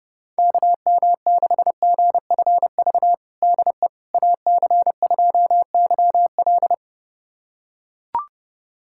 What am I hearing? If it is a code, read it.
Morse code, heard as a keyed tone, about 29 wpm, 700 Hz. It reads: KM6GFV DE AC2YL
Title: USA Callsign DE Callsign 30wpm